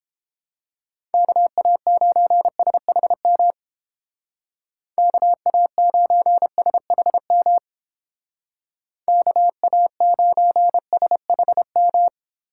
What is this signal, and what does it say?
Morse code, heard as a keyed tone, about 30 wpm, 700 Hz. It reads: KA9SHM KA9SHM KA9SHM